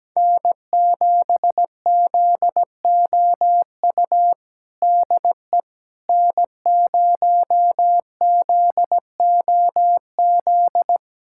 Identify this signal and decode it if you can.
Morse code, heard as a keyed tone, about 17 wpm, 700 Hz. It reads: N7ZOU DE N0ZOZ
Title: USA Callsign DE Callsign - 5 Characters - 17wpm